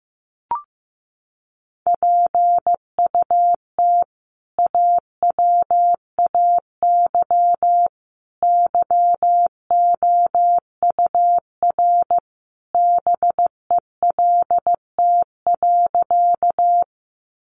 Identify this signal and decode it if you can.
Morse code, heard as a keyed tone, about 15 wpm, 700 Hz. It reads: PUT AWAY YOUR BELT.
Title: Sentences from Top 1400 Words 15wpm